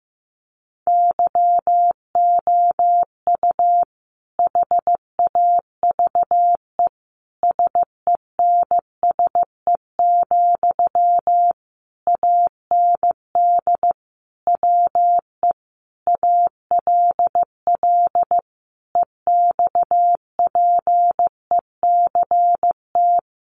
Morse code, heard as a keyed tone, about 15 wpm, 700 Hz. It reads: YOU HAVE SENSE, AND WE ALL EXPECT
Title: Sentences from Top 1100 Words 15wpm